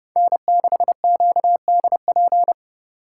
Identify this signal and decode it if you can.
Morse code, heard as a keyed tone, about 31 wpm, 700 Hz. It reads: N6QDP